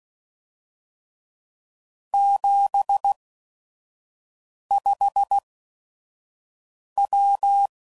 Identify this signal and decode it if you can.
Morse code, heard as a keyed tone, about 16 wpm, spaced as for 5 wpm, 800 Hz. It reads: 75W
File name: CW Code Groups @5WPM / QSOs / QSO WA7QOX 1